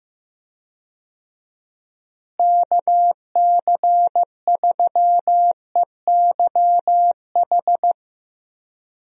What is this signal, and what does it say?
Morse code, heard as a keyed tone, about 15 wpm, 700 Hz. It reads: KC3EYH
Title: USA Call Signs - 6 Characters - Repeated 2x - 15wpm